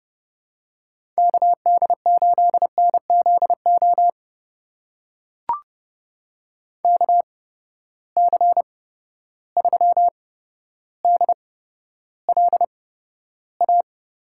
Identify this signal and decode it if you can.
Morse code, heard as a keyed tone, about 30 wpm, spaced as for 13 wpm, 700 Hz. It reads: KD8NZO KC3DLA
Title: USA Call Signs - 6 Characters - 8x Character Spacing - 30wpm